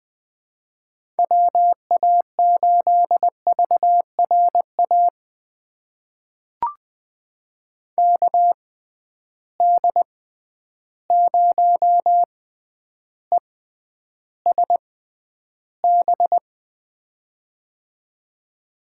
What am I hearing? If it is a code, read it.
Morse code, heard as a keyed tone, about 21 wpm, spaced as for 10 wpm, 700 Hz. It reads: WA8VRA KD0ESB
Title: USA Call Signs - 6 Characters - 6x Character Spacing - 20wpm